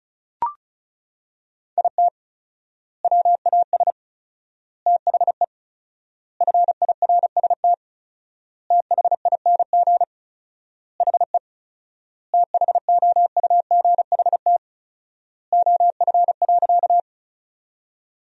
Morse code, heard as a keyed tone, about 36 wpm, 700 Hz. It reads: IT WAS THE FIRST THING HE THOUGHT OF.
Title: 4x Word-Spacing - Sentences from Top 300 Words - 35wpm